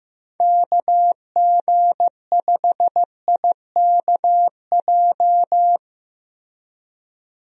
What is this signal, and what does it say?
Morse code, heard as a keyed tone, about 15 wpm, 700 Hz. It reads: KG5IKJ
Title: USA Call Signs - 6 Characters - Repeated 2x - 15wpm